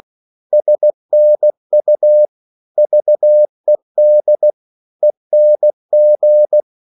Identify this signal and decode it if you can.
Morse code, heard as a keyed tone, about 16 wpm, 600 Hz. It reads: SNU VED ENG
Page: Korte danske ord 16wpm